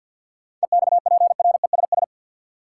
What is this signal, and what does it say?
Morse code, heard as a keyed tone, about 50 wpm, 700 Hz. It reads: EXPRESS
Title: Sentences from Top 1100 Words - New Word Drill - 50wpm